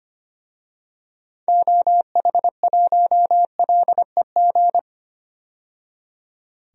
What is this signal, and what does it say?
Morse code, heard as a keyed tone, about 25 wpm, 700 Hz. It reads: OH1LEG
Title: DXCC-100 Call Signs - 6 Characters - 25wpm